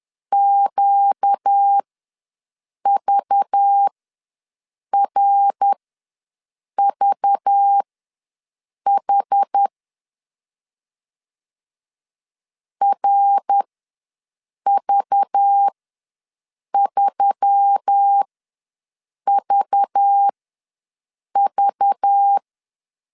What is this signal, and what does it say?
Morse code, heard as a keyed tone, about 10 wpm, spaced as for 6 wpm, 800 Hz. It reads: QVRVH RV3VV
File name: LEZIONI / 5 LBVC / 5 LBV ws 10wpm 3645